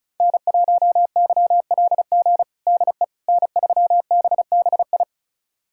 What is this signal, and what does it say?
Morse code, heard as a keyed tone, about 34 wpm, 700 Hz. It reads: N1YLG DE N3BBI
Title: USA Callsign DE Callsign - 5 Characters - 35wpm